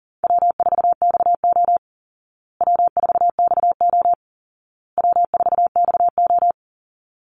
Morse code, heard as a keyed tone, about 40 wpm, 700 Hz. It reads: W4XO W4XO W4XO
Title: Easy Call Signs Repeated 3x - 40wpm